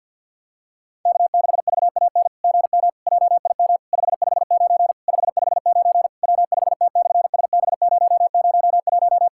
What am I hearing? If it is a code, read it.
Morse code, heard as a keyed tone, about 50 wpm, 700 Hz. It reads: K6VAN GM JIM 559 559 W5T/SB001
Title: SOTA QSO Practice 50wpm